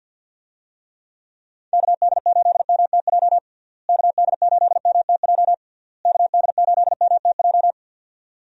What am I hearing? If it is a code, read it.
Morse code, heard as a keyed tone, about 51 wpm, 700 Hz. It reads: KD8MTJ KD8MTJ KD8MTJ
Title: USA Call Signs - 6 Characters - Repeated 3x - 50wpm